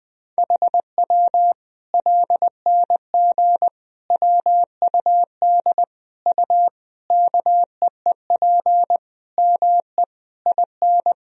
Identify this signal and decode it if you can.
Morse code, heard as a keyed tone, about 21 wpm, 700 Hz. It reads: HW LNG WUD U KEEP ME IN
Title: Sentences from Top 500 Words and Abbreviations 20wpm